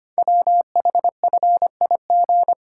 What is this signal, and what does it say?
Morse code, heard as a keyed tone, about 25 wpm, 700 Hz. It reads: WHFIG
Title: Code Groups - 5 Letters - 25wpm